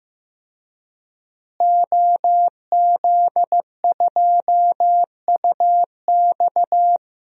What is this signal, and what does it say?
Morse code, heard as a keyed tone, about 15 wpm, 700 Hz. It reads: OZ2UX
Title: DXCC-100 Call Signs - 5 Characters - Repeated 3x - 15wpm